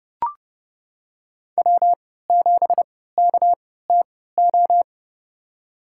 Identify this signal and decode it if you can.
Morse code, heard as a keyed tone, about 30 wpm, spaced as for 17 wpm, 700 Hz. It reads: W7KTO
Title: USA Call Signs - 5 Characters - 3x Character Spacing - 30wpm